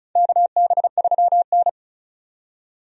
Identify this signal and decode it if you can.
Morse code, heard as a keyed tone, about 36 wpm, 700 Hz. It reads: KB3N
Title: USA Call Signs - 4 Characters - 35wpm